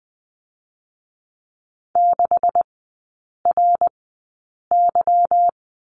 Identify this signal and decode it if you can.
Morse code, heard as a keyed tone, about 20 wpm, spaced as for 8 wpm, 700 Hz. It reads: 6RY
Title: 152 Lesson 25e - Letter p - Three Characters 20wpm